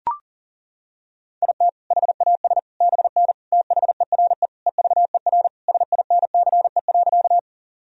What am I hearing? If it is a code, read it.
Morse code, heard as a keyed tone, about 40 wpm, 700 Hz. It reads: IT HAS BN THERE EVER SINCE.
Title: Sentences from Top 400 Words and Abbreviations 40wpm